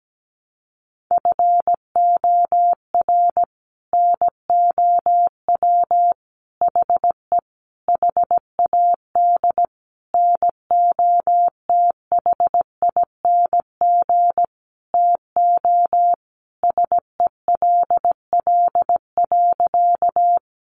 Morse code, heard as a keyed tone, about 17 wpm, 700 Hz. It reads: FOR NOW HE HAD NOTHING TO SELL.
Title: Sentences from Top 1100 Words 17wpm